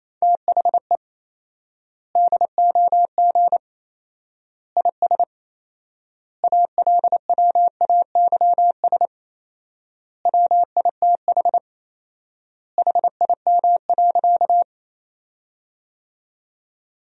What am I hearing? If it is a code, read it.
Morse code, heard as a keyed tone, about 27 wpm, 700 Hz. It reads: THE DOG IS ALWAYS WITH HIM.
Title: 4x Word-Spacing - Sentences from Top 400 Words - 28wpm